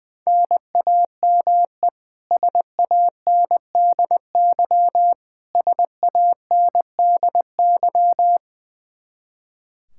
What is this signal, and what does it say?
Morse code, heard as a keyed tone, about 20 wpm, 700 Hz. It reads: NAME SANDY SANDY
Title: K1USN SST QSO Practice 20wpm